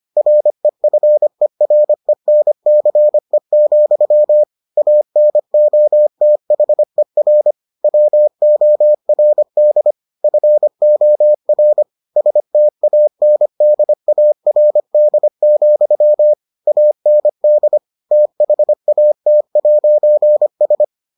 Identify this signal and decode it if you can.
Morse code, heard as a keyed tone, about 25 wpm, 600 Hz. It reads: REFERENCE, ANOTHER WORD FOR STANDARD, AND THAT'S